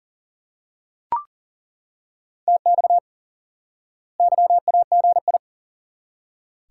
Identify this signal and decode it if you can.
Morse code, heard as a keyed tone, about 37 wpm, 700 Hz. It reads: TX YAGI
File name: IQE Course 3 - Lesson 89 - YAGI - Two Elements - 40wpm